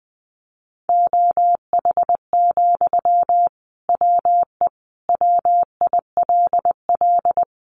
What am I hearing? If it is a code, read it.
Morse code, heard as a keyed tone, about 20 wpm, 700 Hz. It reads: OH, WE WILL
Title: Sentences from Top 1100 Words 20wpm